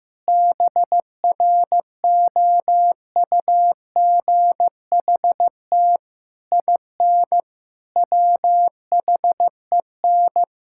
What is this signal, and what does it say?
Morse code, heard as a keyed tone, about 15 wpm, 700 Hz. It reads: BROUGHT IN WHEN
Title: Sentences from Top 1000 Words - Review Part 6 - 15wpm